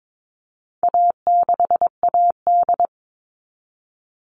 Morse code, heard as a keyed tone, about 22 wpm, 700 Hz. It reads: A6AD
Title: All Cut Numbers - 4-Digit 22wpm